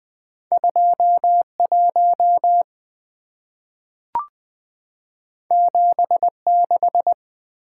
Morse code, heard as a keyed tone, about 20 wpm, 700 Hz. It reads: 21 76
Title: Two Letter-Number Combinations - 20wpm